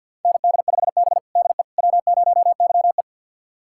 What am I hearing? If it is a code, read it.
Morse code, heard as a keyed tone, about 51 wpm, 700 Hz. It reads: ND5B DE W0YE
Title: USA Callsign DE Callsign - 4 Characters - 50wpm